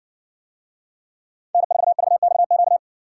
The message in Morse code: N4VXQ

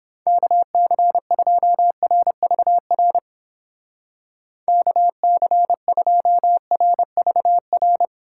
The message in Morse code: KC2RVR KC2RVR

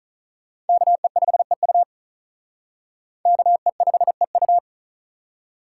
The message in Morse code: KE5EU KE5EU

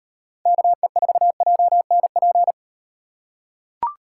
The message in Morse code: KE4JNP

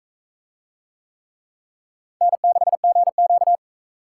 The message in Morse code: N6GQ

